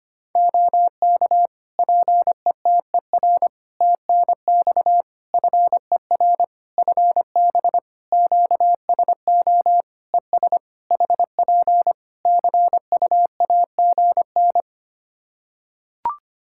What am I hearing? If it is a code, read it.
Morse code, heard as a keyed tone, about 25 wpm, 700 Hz. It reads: OK PETER TNX FER FB QSO ES HP CUAGN
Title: Practice QSO - Exchange 4 - Ending - 25wpm